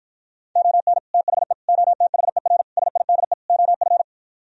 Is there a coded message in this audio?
Yes: Morse code from 0.6 to 4.0 s, heard as a keyed tone, about 52 wpm, 700 Hz. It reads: ON THE OTHER SIDE OF